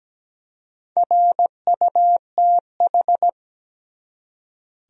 Morse code, heard as a keyed tone, about 17 wpm, 700 Hz. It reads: RUTH
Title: USA Common Names - Max 4 Letters - 17wpm